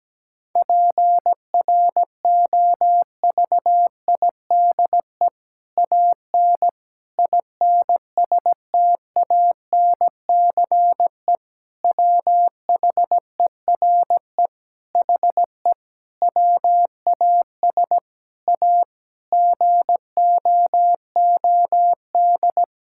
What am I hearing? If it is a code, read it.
Morse code, heard as a keyed tone, about 17 wpm, 700 Hz. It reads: PROVIDE AN INSTANCE WHERE HE WAS A GOOD
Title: Sentences from Top 1000 Words - Review Part 5 - 17wpm